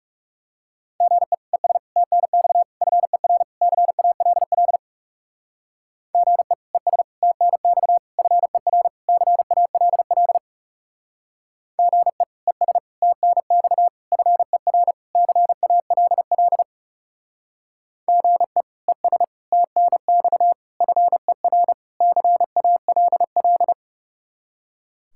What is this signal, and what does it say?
Morse code, heard as a keyed tone, about 36 wpm, 700 Hz. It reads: GE ES TNX FER CALL GE ES TNX FER CALL GE ES TNX FER CALL GE ES TNX FER CALL
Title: Speed-Racing Common QSO Phrases - 30wpm